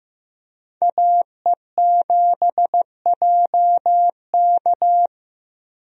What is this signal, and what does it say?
Morse code, heard as a keyed tone, about 15 wpm, 700 Hz. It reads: AE7JK
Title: USA Call Signs - 5 Characters - Repeated 2x - 15wpm